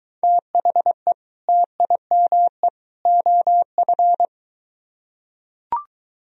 Morse code, heard as a keyed tone, about 22 wpm, 700 Hz. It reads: THE TIME OF